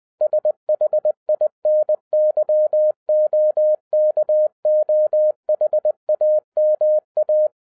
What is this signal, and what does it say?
Morse code, heard as a keyed tone, about 20 wpm, 610 Hz. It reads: SHINYOKOHAMA